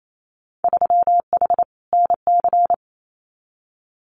Morse code, heard as a keyed tone, about 28 wpm, 700 Hz. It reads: 3H NC
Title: Winter Field Day Exchanges 28wpm